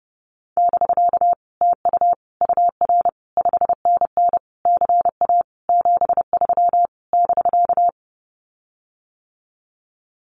Morse code, heard as a keyed tone, about 30 wpm, 700 Hz. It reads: <BK> TU UR 5NN CA 73 <BK>
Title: Common QSO Phrases Encore 30wpm